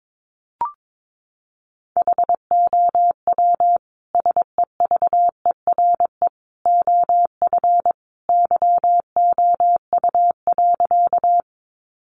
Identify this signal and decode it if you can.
Morse code, heard as a keyed tone, about 22 wpm, 700 Hz. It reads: HOW SEVERE OF YOU.
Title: Sentences from Top 1400 Words 22wpm